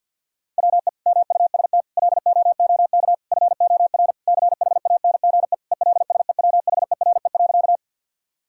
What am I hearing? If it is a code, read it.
Morse code, heard as a keyed tone, about 50 wpm, 700 Hz. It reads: WE MUST LOOK FOR CHANGE ELSEWHERE.
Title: Sentences from Top 1600 Words 50wpm